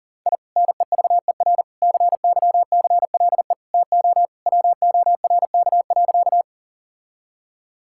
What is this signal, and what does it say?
Morse code, heard as a keyed tone, about 40 wpm, 690 Hz. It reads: I NEVER CYCLE TO WORK.
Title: Sentences from Top 1000 Words - Review Part 6 - 40wpm